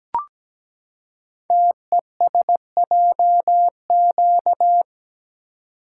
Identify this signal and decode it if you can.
Morse code, heard as a keyed tone, about 17 wpm, 700 Hz. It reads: TESJQ
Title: Code Groups - 5 Letters - 17wpm